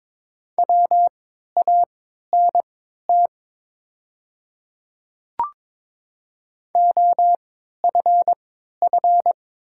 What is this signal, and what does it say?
Morse code, heard as a keyed tone, about 22 wpm, spaced as for 12 wpm, 700 Hz. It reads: WANT OFF
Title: 3x Character Spacing - Top 200 Words - 22wpm